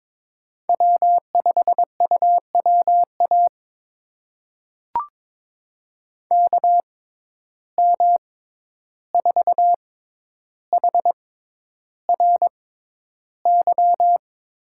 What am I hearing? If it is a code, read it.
Morse code, heard as a keyed tone, about 22 wpm, spaced as for 11 wpm, 700 Hz. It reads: W5UWA KM4HRY
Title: USA Call Signs - 6x Character Spacing - 22wpm